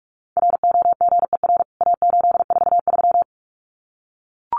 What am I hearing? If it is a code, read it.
Morse code, heard as a keyed tone, about 44 wpm, 700 Hz. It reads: ROGER A843
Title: CWT Exchange 45wpm